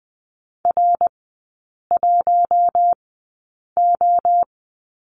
Morse code, heard as a keyed tone, about 20 wpm, spaced as for 8 wpm, 700 Hz. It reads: R1O
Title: 060 Lesson 11e - Letter h - Three Characters 20wpm